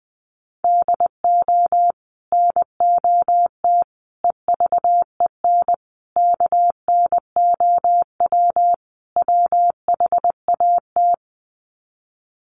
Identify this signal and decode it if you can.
Morse code, heard as a keyed tone, about 20 wpm, 700 Hz. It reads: DO NOT EVEN KNOW WHAT
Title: Sets of 5 Words Encore 20wpm